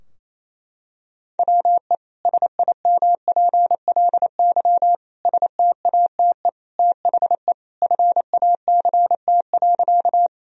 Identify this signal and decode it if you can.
Morse code, heard as a keyed tone, about 28 wpm, 700 Hz. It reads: WE SIMPLY STATE THE FACT.
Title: Sentences from Top 1000 Words - Review Part 4 - 28wpm